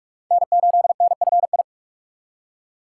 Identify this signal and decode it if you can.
Morse code, heard as a keyed tone, about 46 wpm, 700 Hz. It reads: N8NFI
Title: USA Call Signs - 5 Characters - 4x Character Spacing - 45wpm